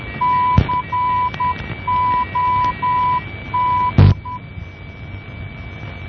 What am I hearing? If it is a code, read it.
Morse code, heard as a keyed tone, about 10 wpm, 990 Hz, with noise. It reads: COT
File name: COD NDB 1988